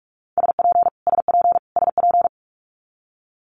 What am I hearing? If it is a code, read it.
Morse code, heard as a keyed tone, about 46 wpm, 700 Hz. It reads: SP SP SP
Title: Multi-letter Phonemes Repeated 3x 45wpm